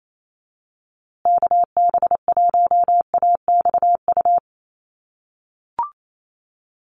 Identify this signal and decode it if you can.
Morse code, heard as a keyed tone, about 28 wpm, 700 Hz. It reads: KB1AXU
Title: USA Call Signs Repeated 3x 28wpm